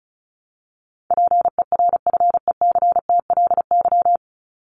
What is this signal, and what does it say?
Morse code, heard as a keyed tone, about 34 wpm, 700 Hz. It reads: PERFECTLY
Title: LY Suffix - Top 100 Words with Base Words 35wpm